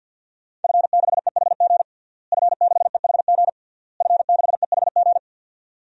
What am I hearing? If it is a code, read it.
Morse code, heard as a keyed tone, about 49 wpm, 700 Hz. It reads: F6EHG F6EHG F6EHG
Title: DXCC-100 Call Signs - 5 Characters - Repeated 3x - 50wpm